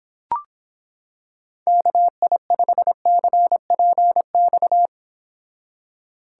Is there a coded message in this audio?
Yes: Morse code from 1.7 to 4.9 s, heard as a keyed tone, about 26 wpm, 700 Hz. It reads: KI5CPX